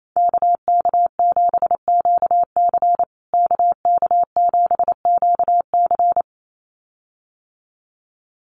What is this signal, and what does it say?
Morse code, heard as a keyed tone, about 28 wpm, 700 Hz. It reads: KK7QC KK7QC
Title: USA Call Signs Repeated 2x 28wpm